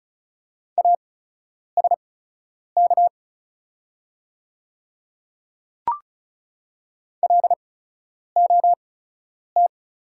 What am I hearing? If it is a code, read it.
Morse code, heard as a keyed tone, about 36 wpm, spaced as for 10 wpm, 690 Hz. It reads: ASK LOT
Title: 8x Character Spacing - Top 500 Words - 35wpm